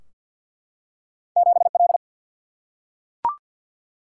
Morse code, heard as a keyed tone, about 49 wpm, 700 Hz. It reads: 7L
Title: Single-Letter Single-Number 50wpm